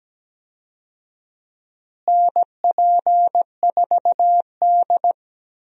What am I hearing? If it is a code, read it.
Morse code, heard as a keyed tone, about 17 wpm, 700 Hz. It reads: NP4D
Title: USA Call Signs - 4 Characters - Repeated 3x - 17wpm